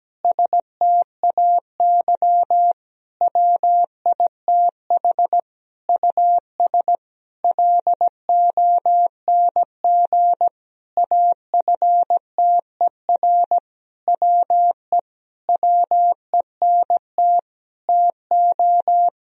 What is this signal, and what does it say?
Morse code, heard as a keyed tone, about 17 wpm, 700 Hz. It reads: STAY WITH US LONG AFTER WE WENT TO
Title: Sentences from Top 1000 Words - Review Part 2 - 17wpm